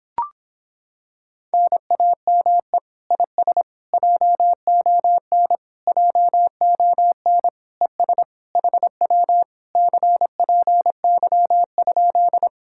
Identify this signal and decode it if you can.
Morse code, heard as a keyed tone, about 26 wpm, 700 Hz. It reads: NAME IS JON JON ES HW CPY?